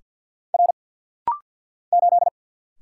Morse code, heard as a keyed tone, about 49 wpm, 700 Hz. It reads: R 8
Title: Single Letter-Number - Rapid-Fire 50wpm